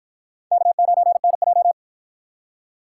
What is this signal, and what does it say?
Morse code, heard as a keyed tone, about 52 wpm, 700 Hz. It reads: K9NJ